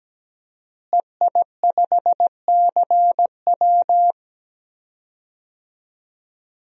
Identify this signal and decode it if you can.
Morse code, heard as a keyed tone, about 17 wpm, 700 Hz. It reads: EI5CW
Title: DXCC-100 Call Signs - 5 Characters - Repeated 2x - 17wpm